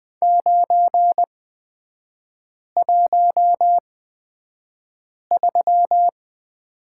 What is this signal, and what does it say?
Morse code, heard as a keyed tone, about 20 wpm, spaced as for 5 wpm, 700 Hz. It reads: 913
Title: 224 Lesson 36e - BT - Three Characters 20wpm